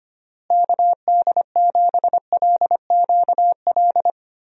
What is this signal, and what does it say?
Morse code, heard as a keyed tone, about 25 wpm, 700 Hz. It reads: KD7LQL
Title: USA Call Signs - 6 Characters - Repeated 2x - 25wpm